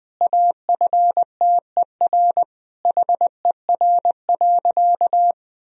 Morse code, heard as a keyed tone, about 21 wpm, 700 Hz. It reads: AFTER HER.
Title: Sentences from Top 1000 Words - Review Part 2 - 20wpm